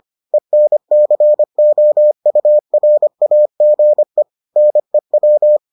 Morse code, heard as a keyed tone, about 25 wpm, 600 Hz. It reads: ENCOURAGE NEW